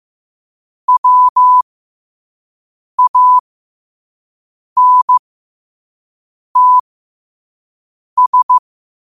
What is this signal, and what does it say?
Morse code, heard as a keyed tone, about 15 wpm, spaced as for 5 wpm, 1000 Hz. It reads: WANTS
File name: qotdfiles / 5wpm